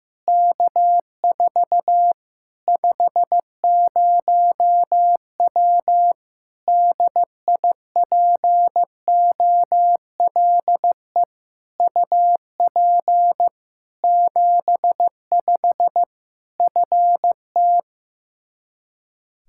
Morse code, heard as a keyed tone, about 15 wpm, 700 Hz. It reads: K4 50W DIPOLE UP 75 FT
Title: Practice QSO - Exchange 2 - Info - 15wpm